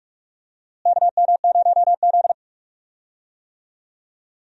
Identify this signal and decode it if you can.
Morse code, heard as a keyed tone, about 45 wpm, 700 Hz. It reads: KM0Z
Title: USA Call Signs - 4 Characters - Repeated 3x - 45wpm